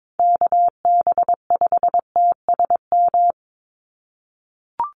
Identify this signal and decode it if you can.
Morse code, heard as a keyed tone, about 22 wpm, 700 Hz. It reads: KB5TSM
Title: USA Call Signs 22wpm